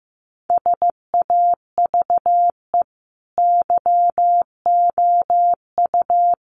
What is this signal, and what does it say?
Morse code, heard as a keyed tone, about 14 wpm, 700 Hz. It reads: SAVE YOU
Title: Sentences from Top 1100 Words 15wpm